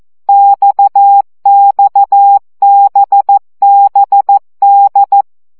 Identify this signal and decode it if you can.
Morse code, heard as a keyed tone, about 14 wpm, 800 Hz. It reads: XXBBD
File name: CW gyakorlas d b x 12 baud